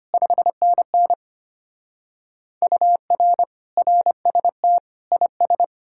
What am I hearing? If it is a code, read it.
Morse code, heard as a keyed tone, about 30 wpm, 700 Hz. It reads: 5NN UR RST IS
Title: Speed-Racing Common QSO Phrases - 25wpm